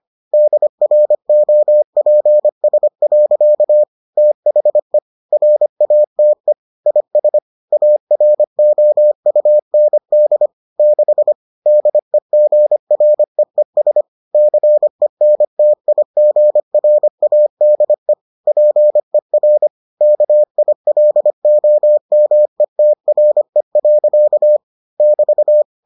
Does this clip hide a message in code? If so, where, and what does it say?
Yes: Morse code from 0.3 to 25.6 s, heard as a keyed tone, about 25 wpm, 600 Hz. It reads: DROPS. THE RATE IS AROUND 6 DEGREES CENTIGRADE PER KILOMETRE. <BT>